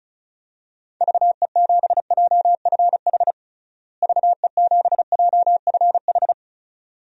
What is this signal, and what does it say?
Morse code, heard as a keyed tone, about 34 wpm, 700 Hz. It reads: VE7JFH VE7JFH
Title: DXCC-100 Call Signs - 6 Characters - Repeated 2x - 35wpm